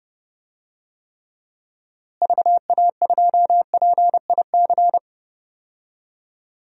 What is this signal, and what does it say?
Morse code, heard as a keyed tone, about 32 wpm, 700 Hz. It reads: VA2PIC
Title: DXCC-100 Call Signs - 6 Characters - Repeated 3x - 30wpm